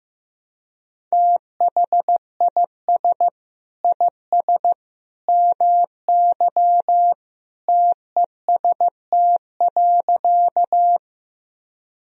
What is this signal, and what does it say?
Morse code, heard as a keyed tone, about 15 wpm, 700 Hz. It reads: THIS IS MY TEST.
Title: Sentences from Top 1000 Words - Review Part 3 - 15wpm